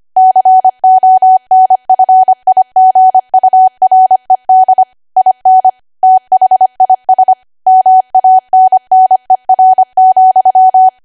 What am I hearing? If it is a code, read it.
Morse code, heard as a keyed tone, about 25 wpm, 750 Hz. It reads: CONFIGURED IN THIS MANNER,
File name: Archive / 25 WPM / 25WPM